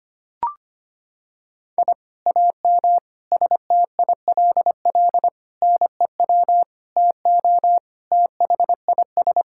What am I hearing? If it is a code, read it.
Morse code, heard as a keyed tone, about 25 wpm, 700 Hz. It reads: I AM STILL NEW TO THIS
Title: Sentences from Top 1000 Words - Review Part 6 - 25wpm